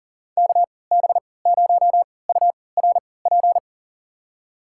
Morse code, heard as a keyed tone, about 40 wpm, spaced as for 23 wpm, 700 Hz. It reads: KB0URP
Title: USA Call Signs - 6 Characters - 3x Character Spacing - 40wpm